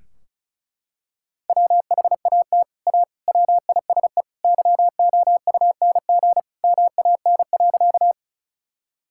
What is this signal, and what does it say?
Morse code, heard as a keyed tone, about 36 wpm, 700 Hz. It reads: WHAT A WISE YOUNG MAN.
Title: Sentences from Top 1600 Words 35wpm